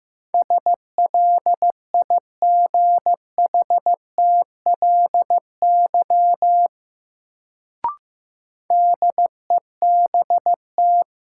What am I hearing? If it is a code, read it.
Morse code, heard as a keyed tone, about 15 wpm, 700 Hz. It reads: SLIGHTLY DEBT
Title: Sentences from Top 1300 Words - New Word Drill Alt - 15wpm